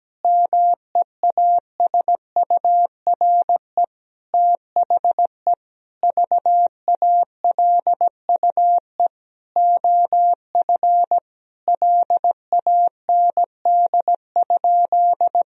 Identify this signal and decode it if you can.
Morse code, heard as a keyed tone, about 17 wpm, 700 Hz. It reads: MEASURE THE VALUE OF LAND?
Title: Sentences from Top 1000 Words - Review Part 3 - 17wpm